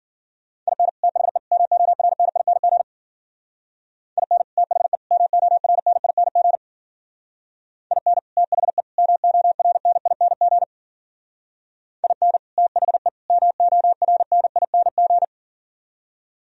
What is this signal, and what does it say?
Morse code, heard as a keyed tone, about 48 wpm, 700 Hz. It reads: IN THE MORNING IN THE MORNING IN THE MORNING IN THE MORNING
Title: Speed-Racing - Sets of 3 Words 40wpm